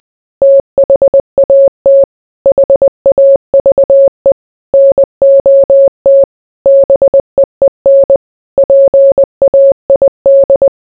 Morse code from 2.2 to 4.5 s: HAVE